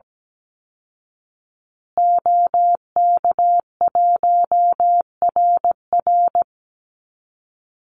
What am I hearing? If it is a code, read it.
Morse code, heard as a keyed tone, about 17 wpm, 700 Hz. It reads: OK1RR
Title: Call Signs Repeated 2x - 17wpm